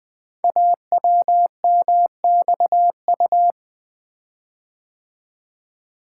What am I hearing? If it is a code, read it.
Morse code, heard as a keyed tone, about 20 wpm, 700 Hz. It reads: AWMXU
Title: Code Groups - 5 Letters - 20wpm